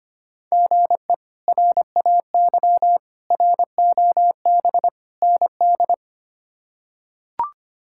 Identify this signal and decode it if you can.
Morse code, heard as a keyed tone, about 25 wpm, 700 Hz. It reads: GE RAY ROB ND
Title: K1USN SST QSO Practice - 25wpm